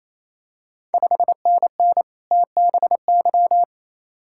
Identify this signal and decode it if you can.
Morse code, heard as a keyed tone, about 28 wpm, 700 Hz. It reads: 5NN TBY
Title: Ontario QSO Party Exchange 28wpm